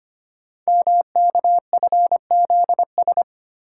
Morse code, heard as a keyed tone, about 25 wpm, 700 Hz. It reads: MKFZS
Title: Code Groups - 5 Letters - 25wpm